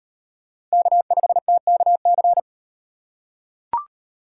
Morse code, heard as a keyed tone, about 36 wpm, 700 Hz. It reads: K5TKC